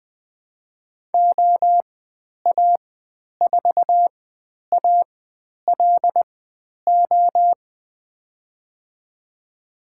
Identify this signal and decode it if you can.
Morse code, heard as a keyed tone, about 20 wpm, spaced as for 10 wpm, 700 Hz. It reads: OA4ALO
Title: 095 Lesson 16g - Letter c - Call Signs 20wpm